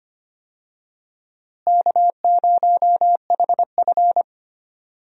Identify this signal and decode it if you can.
Morse code, heard as a keyed tone, about 25 wpm, 700 Hz. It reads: K0HF
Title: USA Call Signs - 4 Characters - Repeated 2x - 25wpm